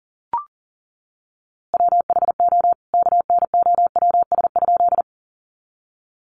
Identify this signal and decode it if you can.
Morse code, heard as a keyed tone, about 40 wpm, 700 Hz. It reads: WHO KNOWS?
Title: A Sound Of Thunder 40wpm